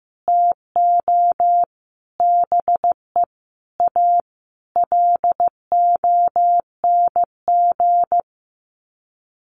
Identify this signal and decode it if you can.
Morse code, heard as a keyed tone, about 15 wpm, 700 Hz. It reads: TO BE A LONG
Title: Sets of 5 Words Encore2 15wpm